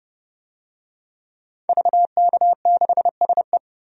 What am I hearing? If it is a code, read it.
Morse code, heard as a keyed tone, about 32 wpm, 700 Hz. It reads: VK6SE
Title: DXCC-100 Call Signs - 5 Characters - 30wpm